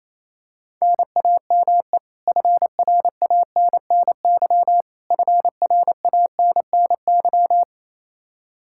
Morse code, heard as a keyed tone, about 28 wpm, 700 Hz. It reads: NAME FRANNY FRANNY
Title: Practice QSO - Exchange 1 - Beginning - 28wpm